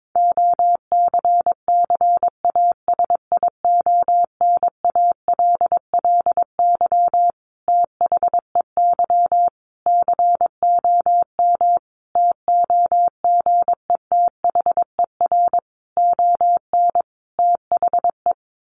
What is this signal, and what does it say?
Morse code, heard as a keyed tone, about 22 wpm, 700 Hz. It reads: OCCASIONALLY THEY COM TOGETHER ON THE
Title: Sentences from Top 1300 Words 22wpm